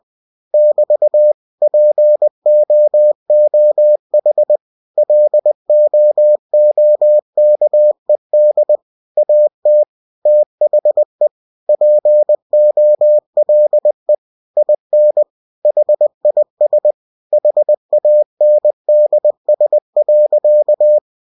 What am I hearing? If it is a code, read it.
Morse code, heard as a keyed tone, about 20 wpm, 600 Hz. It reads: <BT> POOH LOOKED AT THE POLE IN HIS HANDS.
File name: cw-books / winnie-the-pooh / winnie-20wpm-page-0087